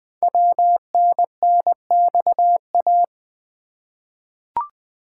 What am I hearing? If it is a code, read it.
Morse code, heard as a keyed tone, about 20 wpm, 700 Hz. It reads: WNNXA
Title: Code Groups - 5 Letters - 20wpm